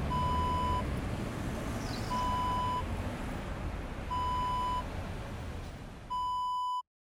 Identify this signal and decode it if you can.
There is no code.